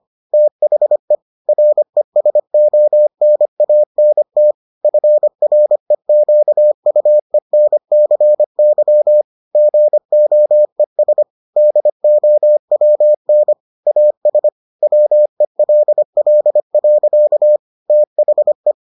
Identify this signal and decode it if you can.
Morse code, heard as a keyed tone, about 25 wpm, 600 Hz. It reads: THE RESONANT FREQUENCY GOES DOWN AS WELL. THE